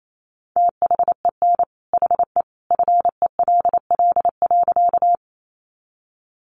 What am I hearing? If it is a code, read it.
Morse code, heard as a keyed tone, about 28 wpm, 700 Hz. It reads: THEN HE FELL.
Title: A Sound Of Thunder 28wpm